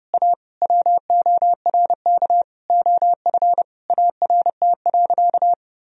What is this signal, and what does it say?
Morse code, heard as a keyed tone, about 30 wpm, 700 Hz. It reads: A WORK OF ART.
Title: Sentences from Top 1000 Words - Review Part 3 - 30wpm